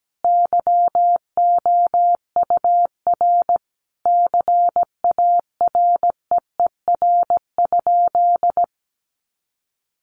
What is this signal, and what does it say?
Morse code, heard as a keyed tone, about 17 wpm, 700 Hz. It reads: YOUR CAREER?
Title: Sentences from Top 1000 Words 17wpm